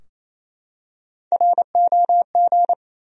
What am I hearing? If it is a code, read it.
Morse code, heard as a keyed tone, about 28 wpm, 700 Hz. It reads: ROG
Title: USA Common Names - Max 3 Letters - 28wpm